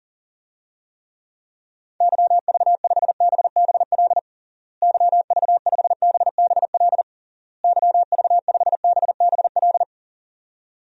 Morse code, heard as a keyed tone, about 40 wpm, 700 Hz. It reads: YV5BBL YV5BBL YV5BBL
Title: DXCC-100 Call Signs - 6 Characters - Repeated 3x - 40wpm